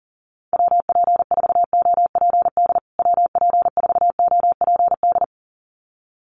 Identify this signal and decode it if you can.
Morse code, heard as a keyed tone, about 40 wpm, 700 Hz. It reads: WP4OPD WP4OPD
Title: USA Call Signs Repeated 2x 40wpm